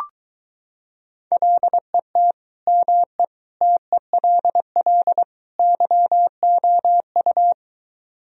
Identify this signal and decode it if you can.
Morse code, heard as a keyed tone, about 22 wpm, 700 Hz. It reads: LET ME TELL YOU